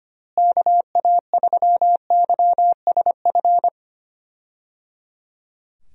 Morse code, heard as a keyed tone, about 25 wpm, 700 Hz. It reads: KA3YSF
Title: USA Call Signs - 6 Characters - Repeated 3x - 25wpm